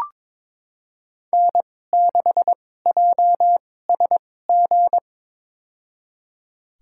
Morse code, heard as a keyed tone, about 22 wpm, spaced as for 16 wpm, 700 Hz. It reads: N6JSG
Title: USA Call Signs - 5 Characters - 2x Character Spacing - 22wpm